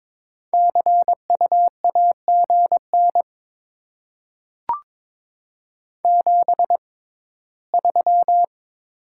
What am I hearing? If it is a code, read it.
Morse code, heard as a keyed tone, about 22 wpm, spaced as for 15 wpm, 700 Hz. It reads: CU AGN 73
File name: Top 20 QSO Elements - 6x Character Spacing - 22wpm